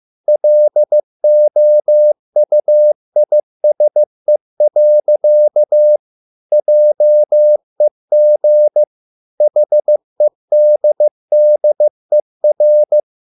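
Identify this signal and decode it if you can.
Morse code, heard as a keyed tone, about 15 wpm, 600 Hz. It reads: LOUISE. JEG HEDDER
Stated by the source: Morse code at a slow 15 wpm from CW med Gnister.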